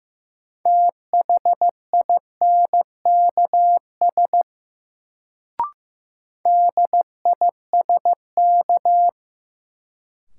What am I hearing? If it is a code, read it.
Morse code, heard as a keyed tone, about 15 wpm, 700 Hz. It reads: THINKS DISK
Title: Sentences from Top 1000 Words - Review Part 6 - New Word Drill - 15wpm